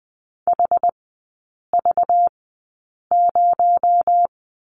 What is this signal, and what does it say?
Morse code, heard as a keyed tone, about 20 wpm, spaced as for 8 wpm, 700 Hz. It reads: HV0
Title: 218 Lesson 35e - Number 0 - Three Characters 20wpm